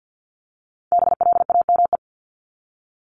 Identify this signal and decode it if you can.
Morse code, heard as a keyed tone, about 50 wpm, 700 Hz. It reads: BLAME
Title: Top 1751-2000 Words 50wpm